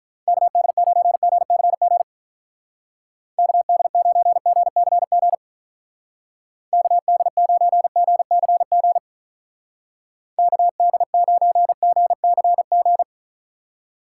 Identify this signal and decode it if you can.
Morse code, heard as a keyed tone, about 44 wpm, 700 Hz. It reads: KD9GCG KD9GCG KD9GCG KD9GCG
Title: Speed Racing - USA 2x3 Call Signs 35wpm